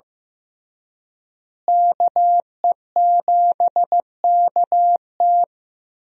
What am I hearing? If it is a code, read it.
Morse code, heard as a keyed tone, about 15 wpm, 700 Hz. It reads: KE7KT
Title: USA Call Signs - 5 Characters - Repeated 3x - 15wpm